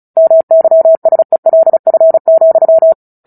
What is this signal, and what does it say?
Morse code, heard as a keyed tone, about 34 wpm, 650 Hz. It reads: MYSELF,
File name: Book Two / 09-Chapter 9 - 35 WPM